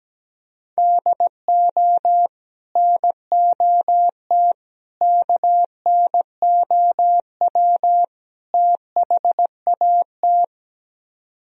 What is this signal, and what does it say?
Morse code, heard as a keyed tone, about 17 wpm, 700 Hz. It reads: DO NOT KNOW THAT
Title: Speed-Racing - Sets of 4 Words 17wpm